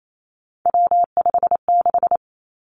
Morse code, heard as a keyed tone, about 28 wpm, 700 Hz. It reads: W56
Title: Three Challenging Letter-Number Combinations 28wpm